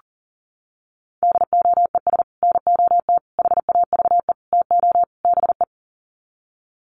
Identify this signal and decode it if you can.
Morse code, heard as a keyed tone, about 40 wpm, 700 Hz. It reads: DOES NOT HAVE TO BE
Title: Sets of 5 Words 40wpm